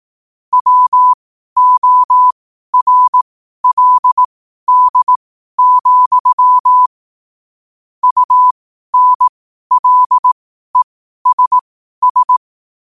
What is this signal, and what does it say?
Morse code, heard as a keyed tone, about 18 wpm, spaced as for 12 wpm, 1000 Hz. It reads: WORLD, UNLESS
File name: qotdfiles / 12wpm